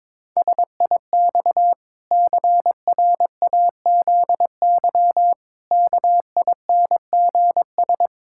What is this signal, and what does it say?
Morse code, heard as a keyed tone, about 22 wpm, 700 Hz. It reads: SIX CRAZY KINGS
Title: Pangram Challenge v2 - 22wpm